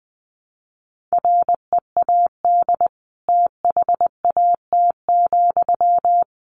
Morse code, heard as a keyed tone, about 20 wpm, 700 Hz. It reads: READ THAT,
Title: Sentences from Top 200 Words 20wpm